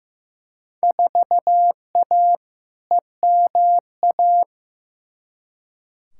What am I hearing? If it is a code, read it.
Morse code, heard as a keyed tone, about 15 wpm, 700 Hz. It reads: 4A EMA
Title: Field Day Exchanges - 15wpm